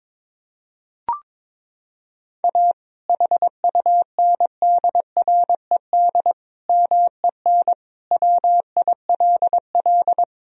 Morse code, heard as a keyed tone, about 22 wpm, 700 Hz. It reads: A HUNDRED MEN WILL
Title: Sentences from Top 1000 Words - Review Part 2 - 22wpm